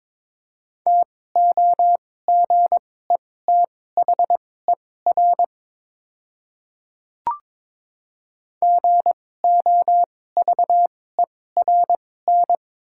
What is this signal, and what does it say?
Morse code, heard as a keyed tone, about 22 wpm, spaced as for 16 wpm, 700 Hz. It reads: TOGETHER GOVERN
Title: 2x Character Spacing - Top 500 Words - 22wpm